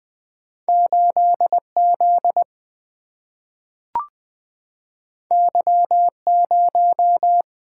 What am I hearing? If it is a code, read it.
Morse code, heard as a keyed tone, about 20 wpm, 700 Hz. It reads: 8 Z Y 0
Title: Single-Letter Single-Number 20wpm